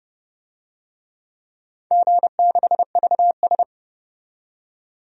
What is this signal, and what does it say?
Morse code, heard as a keyed tone, about 29 wpm, 700 Hz. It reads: G6VS
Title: DXCC-100 Call Signs - 4 Characters - 30wpm